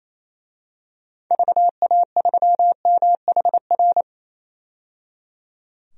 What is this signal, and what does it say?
Morse code, heard as a keyed tone, about 28 wpm, 700 Hz. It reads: VA3MHR
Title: DXCC-100 Call Signs - 6 Characters - 28wpm